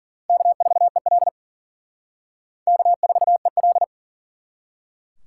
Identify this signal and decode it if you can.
Morse code, heard as a keyed tone, about 47 wpm, 700 Hz. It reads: K4EL K4EL
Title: Speed Racing - USA 1x2 Call Signs 40wpm